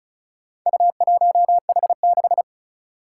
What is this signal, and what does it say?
Morse code, heard as a keyed tone, about 36 wpm, 700 Hz. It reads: U1H6
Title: Four Challenging Letter-Number Combinations 35wpm